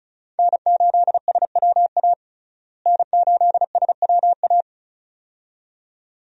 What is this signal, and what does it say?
Morse code, heard as a keyed tone, about 34 wpm, 700 Hz. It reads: N8SWA N8SWA
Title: USA Call Signs - 5 Characters - Repeated 2x - 35wpm